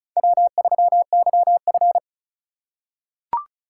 Morse code, heard as a keyed tone, about 36 wpm, 700 Hz. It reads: W3YF
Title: USA Call Signs - 4x Character Spacing - 35wpm